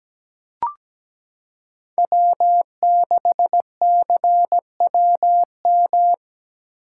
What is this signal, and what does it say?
Morse code, heard as a keyed tone, about 17 wpm, 700 Hz. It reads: W6CWM
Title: K1USN SST QSO Practice - 17wpm